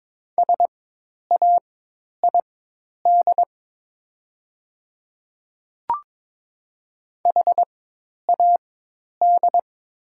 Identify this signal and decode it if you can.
Morse code, heard as a keyed tone, about 22 wpm, spaced as for 10 wpm, 700 Hz. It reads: SAID HAD
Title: 4x Character Spacing - Top 100 Words - 22wpm